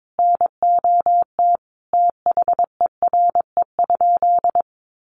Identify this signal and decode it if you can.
Morse code, heard as a keyed tone, about 22 wpm, 700 Hz. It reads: NOT THERE?
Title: Sentences from Top 600 Words 22wpm